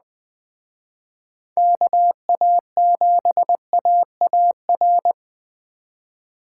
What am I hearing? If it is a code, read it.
Morse code, heard as a keyed tone, about 20 wpm, 700 Hz. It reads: KA7AAR